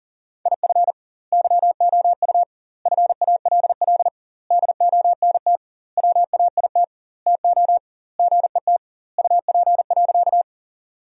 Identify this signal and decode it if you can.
Morse code, heard as a keyed tone, about 40 wpm, 700 Hz. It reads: IF YOU FALL DONT WAIT TO GET UP.
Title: 2x Word-Spacing - Sentences from Top 500 Words - 40wpm